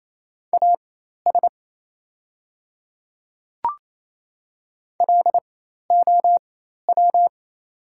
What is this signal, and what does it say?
Morse code, heard as a keyed tone, about 28 wpm, spaced as for 13 wpm, 690 Hz. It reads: AS LOW
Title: 4x Character Spacing - Top 200 Words - 28wpm